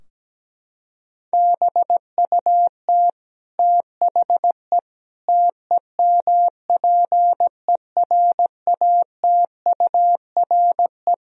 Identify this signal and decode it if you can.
Morse code, heard as a keyed tone, about 17 wpm, 700 Hz. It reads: BUT THE TEMPERATURE
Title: Sentences from Top 1000 Words - Review Part 4 - 17wpm